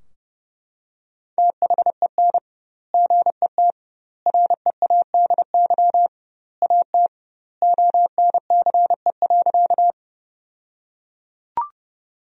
Morse code, heard as a keyed tone, about 30 wpm, 700 Hz. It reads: THEN GET READY AT ONCE.
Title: 2x Word-Spacing - Sentences from Top 400 Words - 30wpm